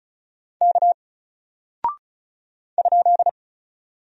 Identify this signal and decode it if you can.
Morse code, heard as a keyed tone, about 36 wpm, 700 Hz. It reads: K ?
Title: Single Character (Letter:Number:Punctuation:Prosign) Alt 35wpm